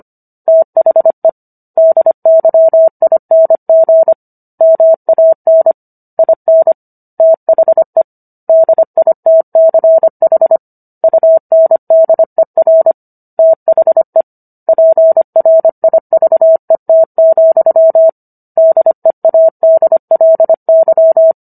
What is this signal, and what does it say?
Morse code, heard as a keyed tone, about 25 wpm, 650 Hz. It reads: THE DYING MAN IN THE DITCH UNDER THE PRIVET, DEADLY